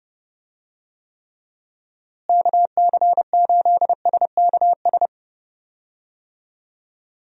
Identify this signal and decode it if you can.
Morse code, heard as a keyed tone, about 32 wpm, 700 Hz. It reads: KC8SKS
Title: USA Call Signs - 6 Characters - Repeated 3x - 30wpm